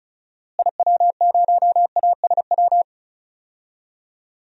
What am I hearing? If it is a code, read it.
Morse code, heard as a keyed tone, about 36 wpm, 700 Hz. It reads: IW0ASW
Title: DXCC-100 Call Signs - 6 Characters - Repeated 2x - 35wpm